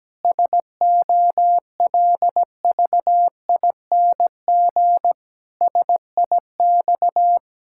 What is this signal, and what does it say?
Morse code, heard as a keyed tone, about 17 wpm, 700 Hz. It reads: SOLVING SIX
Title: Pangram Challenge v2 - 17wpm